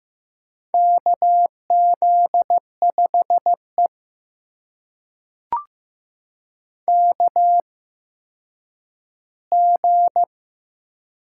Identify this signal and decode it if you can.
Morse code, heard as a keyed tone, about 15 wpm, spaced as for 9 wpm, 700 Hz. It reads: KZ5E KG
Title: USA Call Signs - 8x Character Spacing - 15wpm